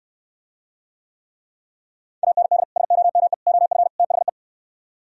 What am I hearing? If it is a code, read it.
Morse code, heard as a keyed tone, about 65 wpm, 700 Hz. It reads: AND SOME OF THE